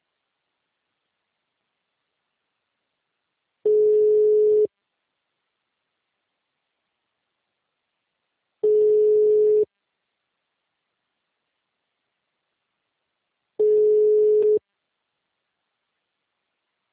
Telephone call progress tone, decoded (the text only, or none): ringback tone